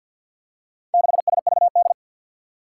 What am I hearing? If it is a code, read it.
Morse code, heard as a keyed tone, about 49 wpm, 700 Hz. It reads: 6SVD
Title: Four Challenging Letter-Number Combinations 50wpm